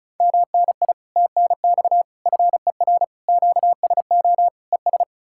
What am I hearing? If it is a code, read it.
Morse code, heard as a keyed tone, about 46 wpm, spaced as for 41 wpm, 700 Hz. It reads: MNI TNX FER QSO ES
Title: Speed-Racing Common QSO Phrases Encore - 35wpm